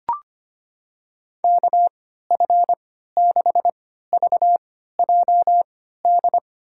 Morse code, heard as a keyed tone, about 25 wpm, spaced as for 14 wpm, 700 Hz. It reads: KF6VJD
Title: USA Call Signs - 6 Characters - 3x Character Spacing - 25wpm